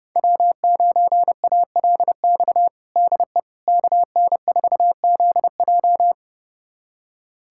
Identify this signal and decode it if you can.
Morse code, heard as a keyed tone, about 30 wpm, 700 Hz. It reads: W9ALX DE KN4ZJ
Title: USA Callsign DE Callsign - 5 Characters - 30wpm